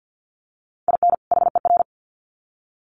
Morse code, heard as a keyed tone, about 50 wpm, 700 Hz. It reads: IN HER
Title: Sets of 2 Words 50wpm